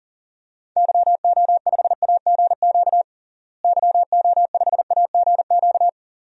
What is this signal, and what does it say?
Morse code, heard as a keyed tone, about 40 wpm, 700 Hz. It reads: YO5AGQ YO5AGQ
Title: DXCC-100 Call Signs - 6 Characters - Repeated 2x - 40wpm